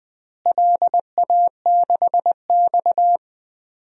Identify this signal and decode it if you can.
Morse code, heard as a keyed tone, about 20 wpm, 700 Hz. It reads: LA6X
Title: DXCC-100 Call Signs - 4 Characters - Repeated 3x - 20wpm